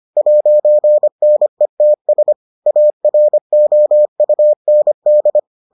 Morse code, heard as a keyed tone, about 25 wpm, 600 Hz. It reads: 'NETS AROUND